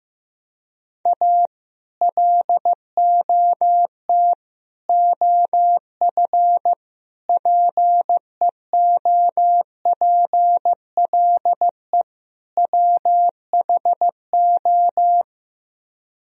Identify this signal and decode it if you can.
Morse code, heard as a keyed tone, about 15 wpm, 700 Hz. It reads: A LOT OF PEOPLE WHO
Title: Speed-Racing - Sets of 5 Words 15wpm